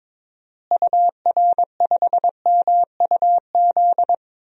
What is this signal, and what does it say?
Morse code, heard as a keyed tone, about 22 wpm, 700 Hz. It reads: UR5MUZ
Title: DXCC-100 Call Signs - 6 Characters - Repeated 2x - 22wpm